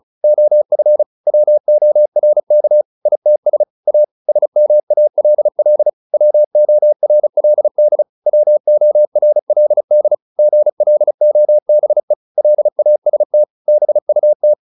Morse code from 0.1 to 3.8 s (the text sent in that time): OF WORK ITS